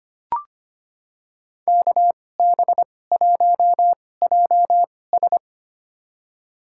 Morse code, heard as a keyed tone, about 25 wpm, spaced as for 18 wpm, 700 Hz. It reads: KB1JS
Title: USA Call Signs - 5 Characters - 2x Character Spacing - 25wpm